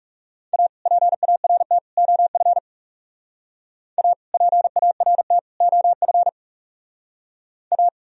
Morse code, heard as a keyed tone, about 34 wpm, 700 Hz. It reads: A PART OF A PART OF A
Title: Speed-Racing - Sets of 3 Words 30wpm